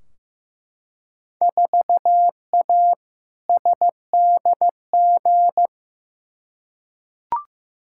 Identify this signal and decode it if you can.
Morse code, heard as a keyed tone, about 15 wpm, 700 Hz. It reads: 4A SDG
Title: Field Day Exchanges - 15wpm